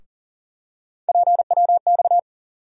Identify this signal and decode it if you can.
Morse code, heard as a keyed tone, about 40 wpm, 700 Hz. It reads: PWX
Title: USA Call Sign Postfixes - 3 Letters 40wpm